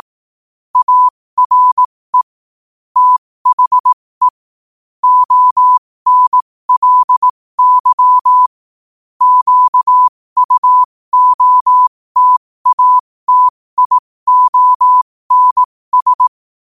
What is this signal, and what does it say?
Morse code, heard as a keyed tone, about 18 wpm, spaced as for 15 wpm, 1000 Hz. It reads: ARE THE ONLY QUOTATIONS
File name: qotdfiles / 15wpm